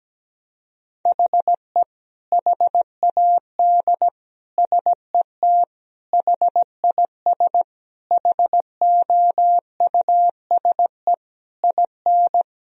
Morse code, heard as a keyed tone, about 17 wpm, 700 Hz. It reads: HE HAD SET HIS HOUSE IN
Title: Sentences from Top 1000 Words - Review Part 2 - 17wpm